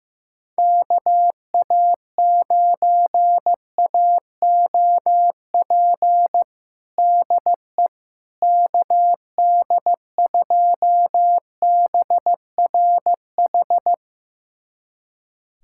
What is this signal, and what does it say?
Morse code, heard as a keyed tone, about 15 wpm, 700 Hz. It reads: KA9AOP DE KD2BRH
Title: USA Callsign DE Callsign - 6 Characters - 15wpm